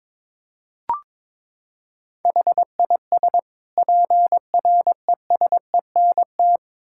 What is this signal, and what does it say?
Morse code, heard as a keyed tone, about 22 wpm, 700 Hz. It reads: HIS PRESENT
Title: Sentences from Top 1000 Words - Review Part 6 - 22wpm